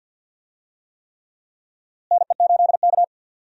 Morse code, heard as a keyed tone, about 52 wpm, 700 Hz. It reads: NE8K
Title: USA Call Signs - 4 Characters - Repeated 3x - 50wpm